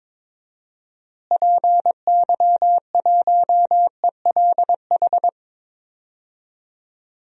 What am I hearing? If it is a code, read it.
Morse code, heard as a keyed tone, about 22 wpm, 700 Hz. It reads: PY1ELH
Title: DXCC-100 Call Signs - 6 Characters - Repeated 3x - 22wpm